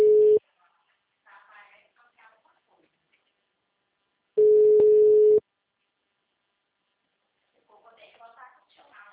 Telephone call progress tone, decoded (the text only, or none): ringback tone